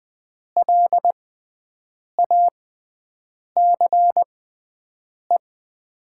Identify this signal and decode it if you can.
Morse code, heard as a keyed tone, about 21 wpm, spaced as for 7 wpm, 700 Hz. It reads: LACE
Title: 6x Character Spacing - Top 300 Words - 20wpm